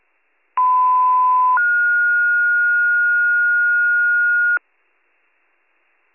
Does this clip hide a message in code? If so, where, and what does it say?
Yes: two-tone page from 0.6 to 4.6 s, heard as a one-second tone, then a three-second tone, about 1010 then 1430 Hz.